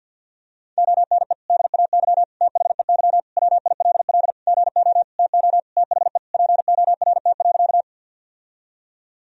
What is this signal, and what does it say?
Morse code, heard as a keyed tone, about 51 wpm, 700 Hz. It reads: ONE DAY THEY WILL GO TO THE PORT.
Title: Sentences from Top 1000 Words - Review Part 1 - 50wpm